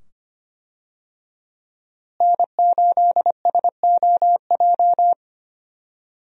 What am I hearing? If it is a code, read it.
Morse code, heard as a keyed tone, about 25 wpm, 700 Hz. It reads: N8SOJ
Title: USA Call Signs - 5 Characters - Repeated 2x - 25wpm